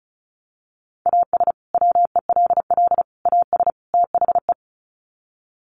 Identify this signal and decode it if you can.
Morse code, heard as a keyed tone, about 34 wpm, 700 Hz. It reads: AS WELL AS THE
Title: Sets of 4 Words 35wpm